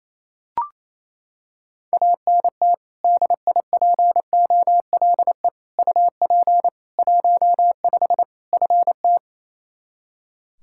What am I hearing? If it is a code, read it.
Morse code, heard as a keyed tone, about 28 wpm, 700 Hz. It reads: ANT DIPOLE UP 15 FT
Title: Practice QSO - Exchange 2 - Info - 28wpm